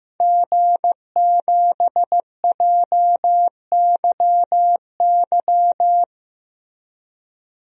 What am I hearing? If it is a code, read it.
Morse code, heard as a keyed tone, about 15 wpm, 700 Hz. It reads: G7JYY
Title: DXCC-100 Call Signs - 5 Characters - Repeated 2x - 15wpm